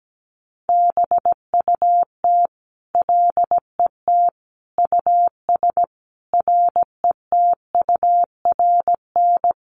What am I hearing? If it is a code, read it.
Morse code, heard as a keyed tone, about 17 wpm, 700 Hz. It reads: BUT LET US RETURN
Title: Sentences from Top 800 Words 17wpm